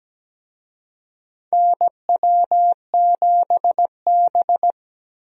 Morse code, heard as a keyed tone, about 17 wpm, 700 Hz. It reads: NW7B
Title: USA Call Signs - 4 Characters - Repeated 2x - 17wpm